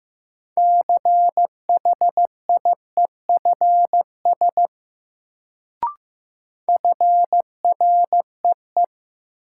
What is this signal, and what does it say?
Morse code, heard as a keyed tone, about 15 wpm, 700 Hz. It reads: CHIEFS FREE
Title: Sets of 5 Words Encore - New Word Drill - 15wpm